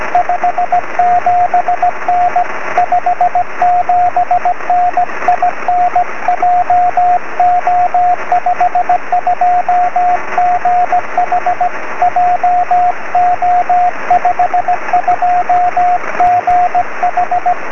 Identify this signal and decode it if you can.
Morse code, heard as a keyed tone, about 17 wpm, 720 Hz, with radio noise. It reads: 57N 57N IN JO52GH JO52GH
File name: sm7lcb / 20 23cm / 2134 dk6aj 23cm